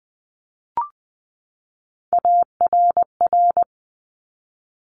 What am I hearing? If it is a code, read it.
Morse code, heard as a keyed tone, about 20 wpm, 700 Hz. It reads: ARR
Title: Top Three Letter Combinations in Words 20wpm